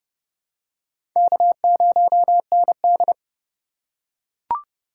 Morse code, heard as a keyed tone, about 30 wpm, 700 Hz. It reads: K0ND
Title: USA Call Signs - 4 Characters - 4x Character Spacing - 30wpm